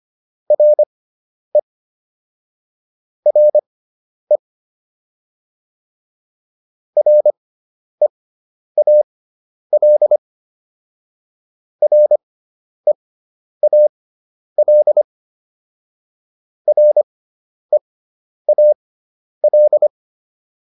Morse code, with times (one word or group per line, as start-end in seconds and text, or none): 0.5-1.6 RE
3.3-4.4 RE
7.0-10.2 REAL
11.8-15.0 REAL
16.7-19.9 REAL